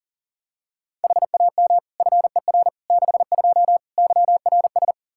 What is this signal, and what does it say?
Morse code, heard as a keyed tone, about 40 wpm, 700 Hz. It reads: HAM FER 62 YRS
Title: K1USN SST QSO Practice 40wpm